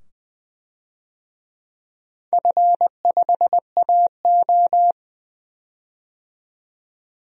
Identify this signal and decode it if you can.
Morse code, heard as a keyed tone, about 20 wpm, 700 Hz. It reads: F5AO
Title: DXCC-100 Call Signs - 4 Characters - 20wpm